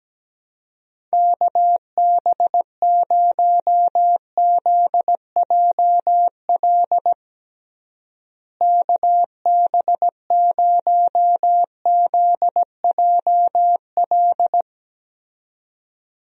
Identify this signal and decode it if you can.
Morse code, heard as a keyed tone, about 17 wpm, 700 Hz. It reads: KB0ZJL KB0ZJL
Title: USA Call Signs - 6 Characters - Repeated 2x - 17wpm